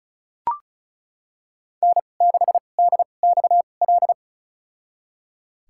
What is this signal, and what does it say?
Morse code, heard as a keyed tone, about 36 wpm, spaced as for 25 wpm, 700 Hz. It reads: N6DXL
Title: USA Call Signs - 5 Characters - 2x Character Spacing - 35wpm